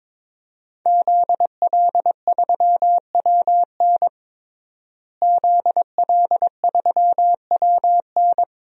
Morse code, heard as a keyed tone, about 22 wpm, 700 Hz. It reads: ZL3WN ZL3WN
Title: DXCC-100 Call Signs - 5 Characters - Repeated 3x - 22wpm